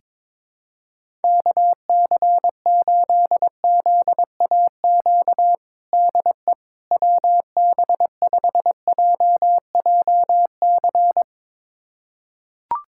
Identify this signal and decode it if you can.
Morse code, heard as a keyed tone, about 22 wpm, 700 Hz. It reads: KC8ZAQ DE WB5JJC
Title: USA Callsign DE Callsign 22wpm